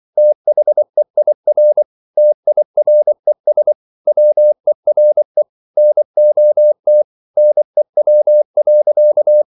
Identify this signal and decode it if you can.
Morse code, heard as a keyed tone, about 24 wpm, 600 Hz. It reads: THEIR TIRES WERE NOT NEW.